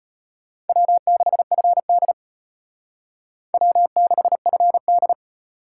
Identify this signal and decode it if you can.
Morse code, heard as a keyed tone, about 32 wpm, 700 Hz. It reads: W6FD W6FD